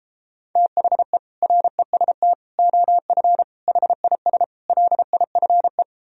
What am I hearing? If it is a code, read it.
Morse code, heard as a keyed tone, about 33 wpm, 700 Hz. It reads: THE REST OF HIS LIFE